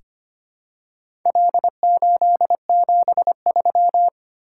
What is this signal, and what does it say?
Morse code, heard as a keyed tone, about 25 wpm, 700 Hz. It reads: L873
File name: practice / 25wpm / Four Letter-Number Combinations - 25wpm